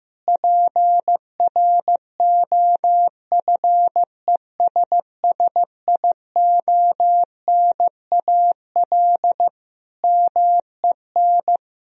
Morse code, heard as a keyed tone, about 15 wpm, 700 Hz. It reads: PROFESSIONAL MEN
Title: Sentences from Top 1000 Words - Review Part 4 - 15wpm